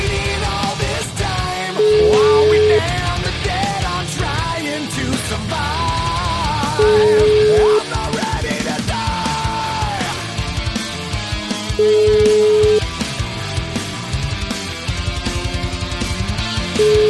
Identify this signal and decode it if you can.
Ringback tone (Europe, 425 Hz), with noise.